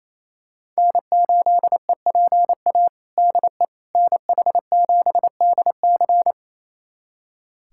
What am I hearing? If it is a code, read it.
Morse code, heard as a keyed tone, about 28 wpm, 700 Hz. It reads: N8EPA DE NH7DC
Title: USA Callsign DE Callsign - 5 Characters - 28wpm